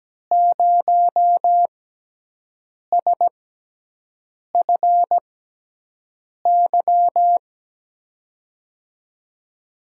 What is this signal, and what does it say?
Morse code, heard as a keyed tone, about 17 wpm, spaced as for 6 wpm, 700 Hz. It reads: 0SFY
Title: USA Call Signs - 6 Characters - 6x Character Spacing - 17wpm